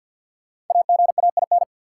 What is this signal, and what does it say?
Morse code, heard as a keyed tone, about 53 wpm, 700 Hz. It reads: AGAIN
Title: Sets of 4 Words - New Word Drill - 50wpm